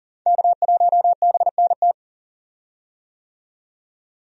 Morse code, heard as a keyed tone, about 41 wpm, 700 Hz. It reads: K1BNT
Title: USA Call Signs - 5 Characters - 40wpm